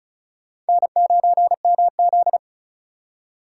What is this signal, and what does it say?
Morse code, heard as a keyed tone, about 34 wpm, 700 Hz. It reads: N9MZ
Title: USA Call Signs - 4 Characters - 3x Character Spacing - 35wpm